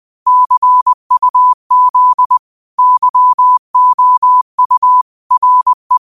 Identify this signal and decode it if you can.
Morse code, heard as a keyed tone, about 20 wpm, 1000 Hz. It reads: CUZ YOU RE
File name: qotdfiles / 20wpm